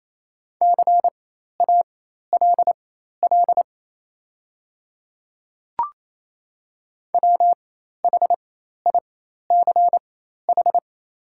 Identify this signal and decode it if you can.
Morse code, heard as a keyed tone, about 28 wpm, spaced as for 13 wpm, 700 Hz. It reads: CALL WHICH
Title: 4x Character Spacing - Top 100 Words - 28wpm